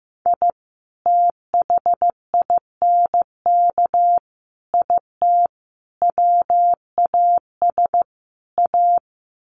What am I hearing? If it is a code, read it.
Morse code, heard as a keyed tone, about 15 wpm, 700 Hz. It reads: I THINK IT WAS A
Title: Sets of 5 Words 15wpm